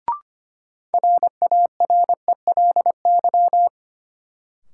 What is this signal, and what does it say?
Morse code, heard as a keyed tone, about 25 wpm, 700 Hz. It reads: RARELY
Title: Sentences from Top 1400 Words - New Word Drill - 25wpm